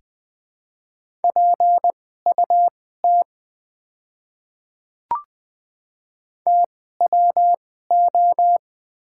Morse code, heard as a keyed tone, about 20 wpm, spaced as for 14 wpm, 700 Hz. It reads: PUT TWO
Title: 2x Character Spacing - Top 300 Words - 20wpm